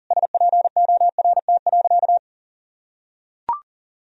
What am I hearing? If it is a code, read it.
Morse code, heard as a keyed tone, about 40 wpm, 700 Hz. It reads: SPORT.
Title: Sentences from Top 1000 Words - Review Part 4 - 40wpm